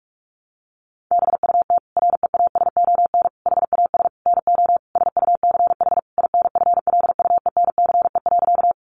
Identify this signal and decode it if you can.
Morse code, heard as a keyed tone, about 44 wpm, 700 Hz. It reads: BUT REASON HAS NO SUCH INFLUENCE.
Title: Sentences from Top 1100 Words 45wpm